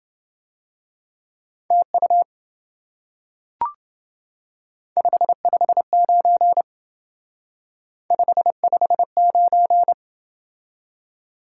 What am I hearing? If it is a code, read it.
Morse code, heard as a keyed tone, about 28 wpm, 700 Hz. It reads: TU 559 559